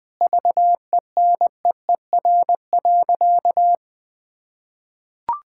Morse code, heard as a keyed tone, about 21 wpm, 700 Hz. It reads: VENEER.
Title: Pangram Challenge v2 - 20wpm